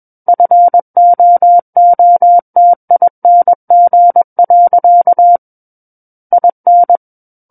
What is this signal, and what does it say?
Morse code, heard as a keyed tone, about 21 wpm, 700 Hz. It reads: FOOTING. IN
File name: Book One / 12-Chapter 12 - 21 WPM